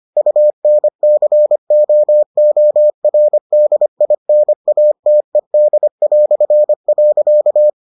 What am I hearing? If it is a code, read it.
Morse code, heard as a keyed tone, about 25 wpm, 600 Hz. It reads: UNCOORDINATED".